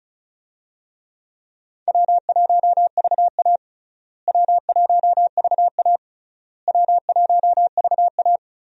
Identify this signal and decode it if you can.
Morse code, heard as a keyed tone, about 34 wpm, 700 Hz. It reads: W1VA W1VA W1VA
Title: USA Call Signs - 4 Characters - Repeated 3x - 35wpm